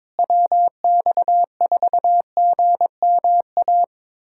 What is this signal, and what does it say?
Morse code, heard as a keyed tone, about 22 wpm, 700 Hz. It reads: WX4GMA
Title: USA Callsign DE Callsign - 6 Characters - 22wpm